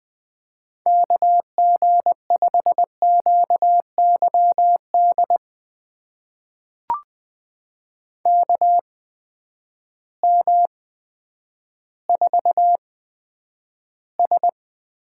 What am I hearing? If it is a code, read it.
Morse code, heard as a keyed tone, about 20 wpm, spaced as for 10 wpm, 700 Hz. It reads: KG5QYD KM4S
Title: USA Call Signs - 6 Characters - 8x Character Spacing - 20wpm